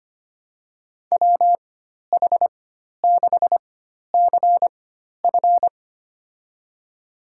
Morse code, heard as a keyed tone, about 25 wpm, spaced as for 12 wpm, 700 Hz. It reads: WH6CF
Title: USA Call Signs - 5 Characters - 4x Character Spacing - 25wpm